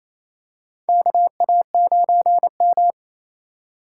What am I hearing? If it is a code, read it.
Morse code, heard as a keyed tone, about 28 wpm, 700 Hz. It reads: KA9M
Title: USA Call Signs - 4 Characters - 3x Character Spacing - 28wpm